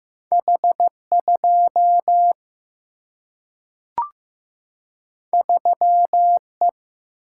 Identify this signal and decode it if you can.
Morse code, heard as a keyed tone, about 15 wpm, 700 Hz. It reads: H 2 3 E
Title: Single-Letter Single-Number 15wpm